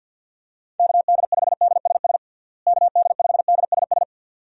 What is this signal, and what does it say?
Morse code, heard as a keyed tone, about 50 wpm, 700 Hz. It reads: KD5DSS KD5DSS
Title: USA Call Signs - 6 Characters - Repeated 2x - 50wpm